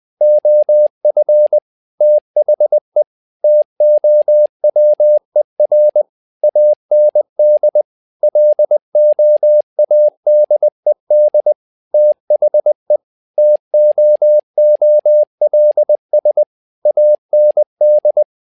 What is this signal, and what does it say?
Morse code, heard as a keyed tone, about 20 wpm, 600 Hz. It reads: OF THE TOWER AND LOADED THE TOOLS AND
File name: tower trouble 20wpm